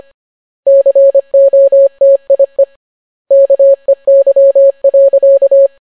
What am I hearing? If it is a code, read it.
Morse code, heard as a keyed tone, about 25 wpm, 550 Hz. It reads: COTIE KEY.